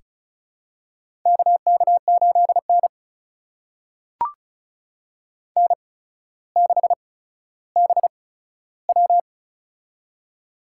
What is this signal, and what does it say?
Morse code, heard as a keyed tone, about 36 wpm, spaced as for 15 wpm, 700 Hz. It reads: KK8N N6BW
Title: USA Call Signs - 4 Characters - 8x Character Spacing - 35wpm